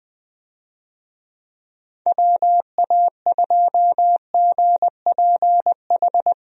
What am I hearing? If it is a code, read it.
Morse code, heard as a keyed tone, about 20 wpm, 700 Hz. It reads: WA2GPH
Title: USA Call Signs - 6 Characters - Repeated 3x - 20wpm